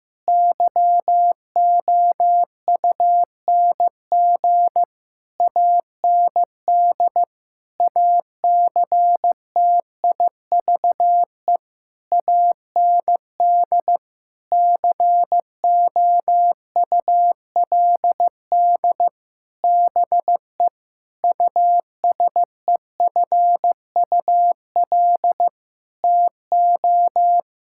Morse code, heard as a keyed tone, about 15 wpm, 700 Hz. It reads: YOUNG AND ACTIVE AND COULD BE USEFUL TO
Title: Sentences from Top 1000 Words - Review Part 6 - 15wpm